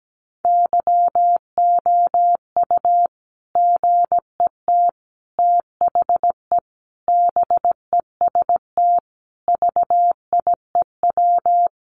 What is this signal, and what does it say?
Morse code, heard as a keyed tone, about 17 wpm, 700 Hz. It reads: YOU GET THE BEST VIEW
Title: Sentences from Top 1200 Words 17wpm